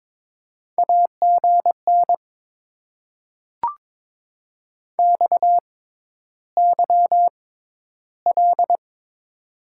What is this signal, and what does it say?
Morse code, heard as a keyed tone, about 21 wpm, spaced as for 11 wpm, 700 Hz. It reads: AGN XYL
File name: Top 90 QSO Elements - 6x Character Spacing - 22wpm